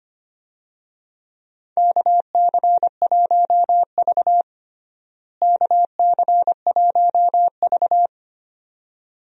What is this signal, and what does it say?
Morse code, heard as a keyed tone, about 25 wpm, 700 Hz. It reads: KC1V KC1V
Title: USA Call Signs - 4 Characters - Repeated 2x - 25wpm